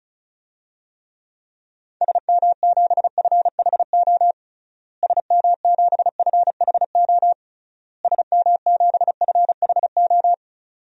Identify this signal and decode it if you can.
Morse code, heard as a keyed tone, about 36 wpm, 700 Hz. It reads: SM7FHO SM7FHO SM7FHO
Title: DXCC-100 Call Signs - 6 Characters - Repeated 3x - 35wpm